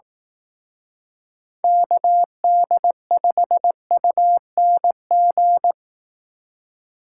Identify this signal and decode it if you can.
Morse code, heard as a keyed tone, about 18 wpm, 700 Hz. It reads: KD5UNG
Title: Speed Racing - USA 2x3 Call Signs 15wpm